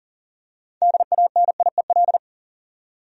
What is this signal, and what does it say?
Morse code, heard as a keyed tone, about 40 wpm, 700 Hz. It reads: DANIEL
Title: USA Common Names - 40wpm